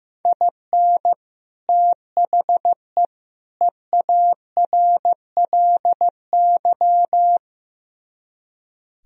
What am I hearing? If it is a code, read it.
Morse code, heard as a keyed tone, about 15 wpm, 700 Hz. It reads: IN THE EARLY
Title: Speed-Racing - Sets of 3 Words 15wpm